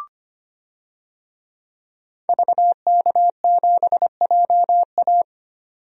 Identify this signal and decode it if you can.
Morse code, heard as a keyed tone, about 25 wpm, 700 Hz. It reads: VK7JA
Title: DXCC-100 Call Signs - 5 Characters - 25wpm